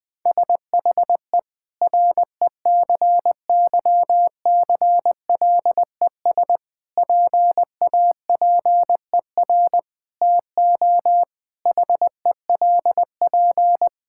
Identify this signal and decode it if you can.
Morse code, heard as a keyed tone, about 20 wpm, 700 Hz. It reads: SHE RECYCLES PAPER TO HELP
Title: Difficult Sentences with Hints - 20wpm